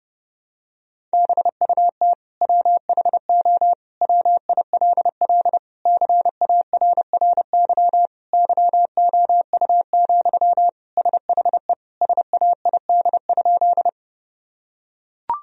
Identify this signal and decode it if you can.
Morse code, heard as a keyed tone, about 30 wpm, 700 Hz. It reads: BUT WHO WILL CARRY YOU, SHE SAID?
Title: Sentences from Top 1000 Words - Review Part 1 - 30wpm